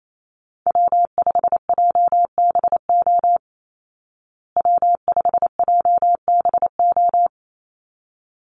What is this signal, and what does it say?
Morse code, heard as a keyed tone, about 28 wpm, 700 Hz. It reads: W5JBO W5JBO
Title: Call Signs Repeated 3x - 28wpm